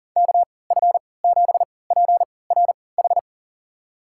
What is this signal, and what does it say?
Morse code, heard as a keyed tone, about 40 wpm, spaced as for 23 wpm, 700 Hz. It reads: KF7PRH
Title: USA Call Signs - 6 Characters - 3x Character Spacing - 40wpm